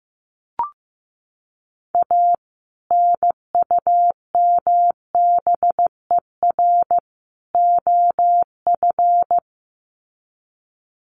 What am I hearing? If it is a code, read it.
Morse code, heard as a keyed tone, about 15 wpm, 700 Hz. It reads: A NUMBER OF
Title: Sets of 3 Words 15wpm